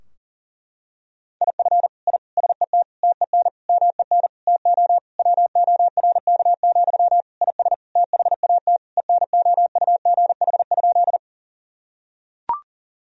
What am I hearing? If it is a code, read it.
Morse code, heard as a keyed tone, about 40 wpm, 700 Hz. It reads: IF I SET TEN MEN TO WORK, IS THAT ENOUGH?
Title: Sentences from Top 1000 Words - Review Part 2 - 40wpm